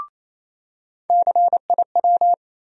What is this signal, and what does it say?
Morse code, heard as a keyed tone, about 28 wpm, 700 Hz. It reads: CIW
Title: USA Call Sign Postfixes - 3 Letters 28wpm